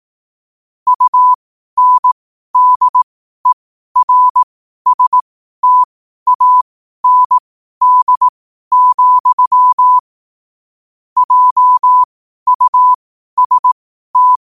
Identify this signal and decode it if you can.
Morse code, heard as a keyed tone, about 18 wpm, spaced as for 12 wpm, 1000 Hz. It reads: UNDERSTAND, JUST
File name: qotdfiles / 12wpm